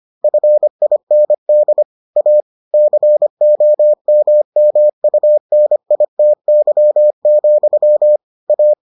Morse code, heard as a keyed tone, about 25 wpm, 600 Hz. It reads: FIND A COMMUNITY, A